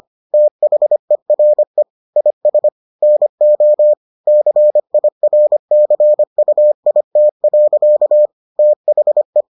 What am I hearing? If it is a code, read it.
Morse code, heard as a keyed tone, about 25 wpm, 600 Hz. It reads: THERE IS NO CIRCUIT. THE